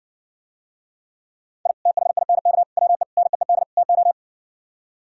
Morse code, heard as a keyed tone, about 70 wpm, 700 Hz. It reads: I THINK WE NEED TO